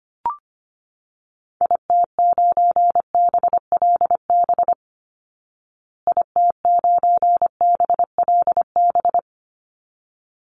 Morse code, heard as a keyed tone, about 25 wpm, 700 Hz. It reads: IT9BLB IT9BLB
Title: Call Signs Repeated 3x - 25wpm